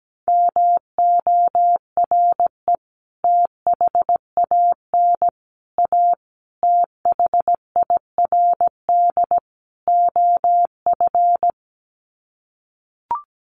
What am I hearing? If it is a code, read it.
Morse code, heard as a keyed tone, about 17 wpm, 700 Hz. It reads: MORE THAN A THIRD OF
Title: Sets of 5 Words Encore2 17wpm